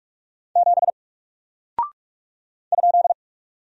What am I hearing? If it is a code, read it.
Morse code, heard as a keyed tone, about 44 wpm, 700 Hz. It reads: 7 ?
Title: Single Character (Letter:Number:Punctuation:Prosign) Alt 45wpm